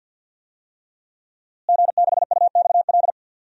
Morse code, heard as a keyed tone, about 48 wpm, 700 Hz. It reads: G6UXL
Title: DXCC-100 Call Signs - 5 Characters - Repeated 3x - 50wpm